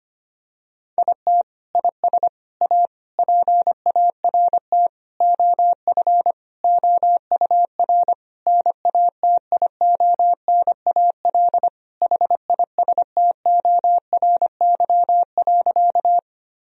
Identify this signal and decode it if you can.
Morse code, heard as a keyed tone, about 25 wpm, 700 Hz. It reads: IT IS A PART OF OUR NATIONAL HISTORY.
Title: Sentences from Top 1000 Words - Review Part 4 - 25wpm